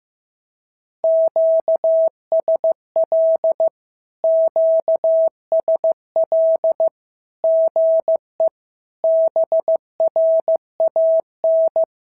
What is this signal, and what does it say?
Morse code, heard as a keyed tone, about 15 wpm, 650 Hz. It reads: QSL QSL GE BRAN
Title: SOTA QSO Practice 15wpm